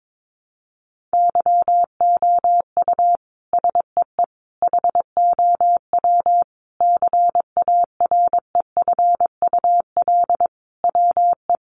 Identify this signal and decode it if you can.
Morse code, heard as a keyed tone, about 22 wpm, 700 Hz. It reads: YOU SEE HOW CAREFUL WE
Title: A Sound Of Thunder 22wpm